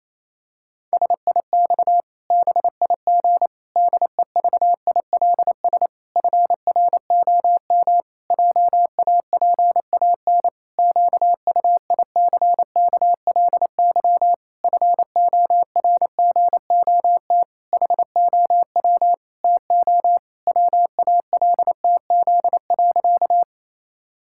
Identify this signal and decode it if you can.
Morse code, heard as a keyed tone, about 28 wpm, 700 Hz. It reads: SIX BIG DEVILS FROM JAPAN QUICKLY FORGOT HOW TO WALTZ.
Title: Pangram Challenge v2 - 28wpm